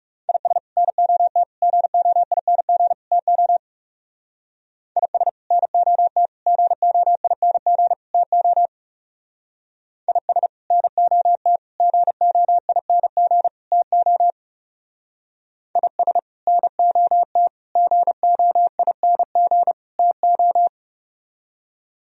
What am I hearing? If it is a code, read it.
Morse code, heard as a keyed tone, about 35 wpm, 700 Hz. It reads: IS NOT GOING TO IS NOT GOING TO IS NOT GOING TO IS NOT GOING TO
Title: Speed-Racing - Sets of 4 Words 30wpm